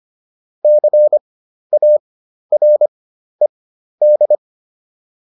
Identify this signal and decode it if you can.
Morse code, heard as a keyed tone, about 25 wpm, spaced as for 12 wpm, 600 Hz. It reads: CARED